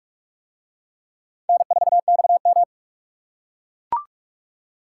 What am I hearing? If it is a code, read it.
Morse code, heard as a keyed tone, about 42 wpm, 700 Hz. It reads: N4XM